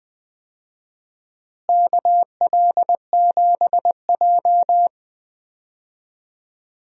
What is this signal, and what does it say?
Morse code, heard as a keyed tone, about 20 wpm, 700 Hz. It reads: KL7J
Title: DXCC-100 Call Signs - 4 Characters - Repeated 3x - 20wpm